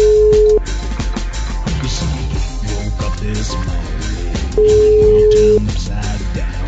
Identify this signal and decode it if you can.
Ringback tone (Europe, 425 Hz).